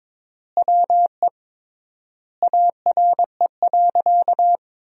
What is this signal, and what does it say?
Morse code, heard as a keyed tone, about 22 wpm, 700 Hz. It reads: WE ARE.
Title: 3x Word-Spacing - Sentences from Top 200 Words - 22wpm